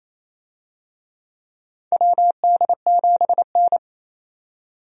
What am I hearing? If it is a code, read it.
Morse code, heard as a keyed tone, about 27 wpm, 700 Hz. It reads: WD7N
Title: USA Call Signs - 4 Characters - 28wpm